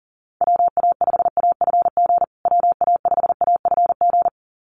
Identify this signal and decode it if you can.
Morse code, heard as a keyed tone, about 37 wpm, 700 Hz. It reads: WA5AFG WA5AFG
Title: USA Call Signs Repeated 2x 40wpm